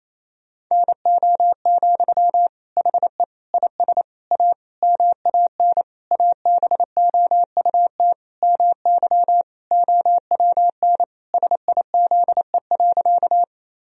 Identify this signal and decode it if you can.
Morse code, heard as a keyed tone, about 27 wpm, 700 Hz. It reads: NO, HE IS A MAN ABOUT MY OWN SIZE.
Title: Sentences from Top 1000 Words - Review Part 2 - 28wpm